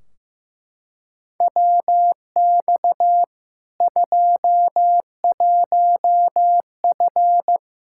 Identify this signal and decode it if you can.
Morse code, heard as a keyed tone, about 15 wpm, 700 Hz. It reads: WX 21F
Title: Practice QSO - Exchange 2 - Info - 15wpm